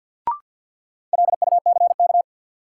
Morse code, heard as a keyed tone, about 53 wpm, 700 Hz. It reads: LUCK
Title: Sentences from Top 1500 Words - New Word Drill Alt - 50wpm